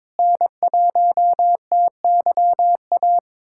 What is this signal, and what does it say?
Morse code, heard as a keyed tone, about 22 wpm, 700 Hz. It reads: N1TYA
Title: USA Callsign DE Callsign - 5 Characters - 22wpm